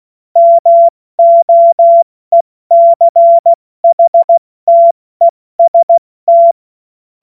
Morse code, heard as a keyed tone, about 16 wpm, 680 Hz. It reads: MOECHTEST
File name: Morsekurs / Witze / Witz 24. Januar 2023 70 F80